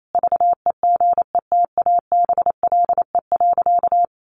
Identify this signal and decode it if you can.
Morse code, heard as a keyed tone, about 28 wpm, 700 Hz. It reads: VEGETABLE.
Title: Sentences from Top 1300 Words 28wpm